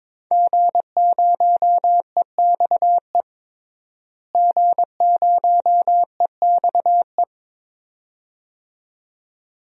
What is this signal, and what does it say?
Morse code, heard as a keyed tone, about 22 wpm, 700 Hz. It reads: G0EXE G0EXE
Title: DXCC-100 Call Signs - 5 Characters - Repeated 2x - 22wpm